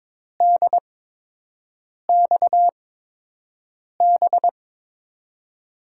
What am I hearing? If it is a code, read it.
Morse code, heard as a keyed tone, about 22 wpm, spaced as for 6 wpm, 700 Hz. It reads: DXB
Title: USA Call Signs - 6 Characters - 8x Character Spacing - 22wpm